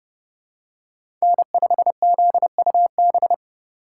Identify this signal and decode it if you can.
Morse code, heard as a keyed tone, about 30 wpm, 700 Hz. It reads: N5ZUB
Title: USA Call Signs - 5 Characters - 3x Character Spacing - 30wpm